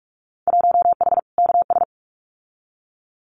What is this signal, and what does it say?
Morse code, heard as a keyed tone, about 46 wpm, 700 Hz. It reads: 1H KS
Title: Winter Field Day Exchanges 45wpm